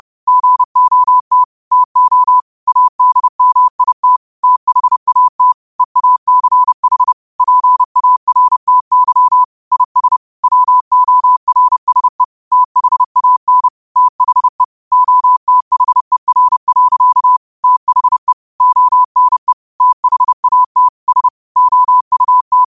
Morse code, heard as a keyed tone, about 30 wpm, 1000 Hz. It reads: GOT TO ADMIT THAT EACH PARTY IS WORSE THAN THE OTHER. THE ONE THAT S OUT
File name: qotdfiles / 30wpm